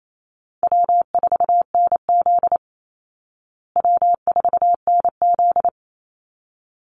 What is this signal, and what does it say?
Morse code, heard as a keyed tone, about 28 wpm, 700 Hz. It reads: W4NZ W4NZ
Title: Easy Call Signs Repeated 2x - 28wpm